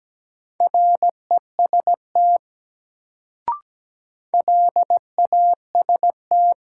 Morse code, heard as a keyed tone, about 17 wpm, 700 Hz. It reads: REST LAST
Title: Sets of 2 Words - New Word Drill - 17wpm